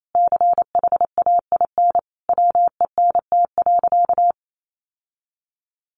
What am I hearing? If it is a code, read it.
Morse code, heard as a keyed tone, about 28 wpm, 700 Hz. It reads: CHAIN WENT.
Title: Sentences from Top 1500 Words 28wpm